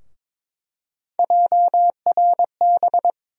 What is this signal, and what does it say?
Morse code, heard as a keyed tone, about 22 wpm, 700 Hz. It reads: JRB
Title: USA Call Sign Postfixes - 3 Letters 22wpm